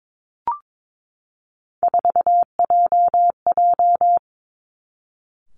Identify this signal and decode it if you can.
Morse code, heard as a keyed tone, about 22 wpm, 700 Hz. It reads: 4JJ
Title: Three Challenging Letter-Number Combinations 22wpm